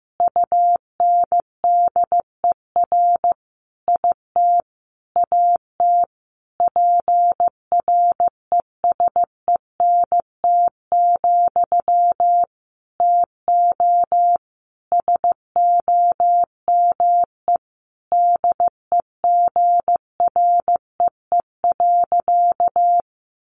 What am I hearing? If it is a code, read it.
Morse code, heard as a keyed tone, about 15 wpm, 700 Hz. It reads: UNDER IT AT PRESENT, TO SOME DEGREE.
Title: Sentences from Top 1500 Words 15wpm